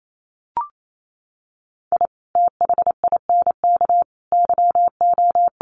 Morse code, heard as a keyed tone, about 28 wpm, 700 Hz. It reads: I THINK YO
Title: Sets of 5 Words 28wpm